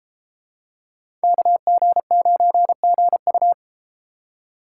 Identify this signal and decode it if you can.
Morse code, heard as a keyed tone, about 31 wpm, 700 Hz. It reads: KG9GU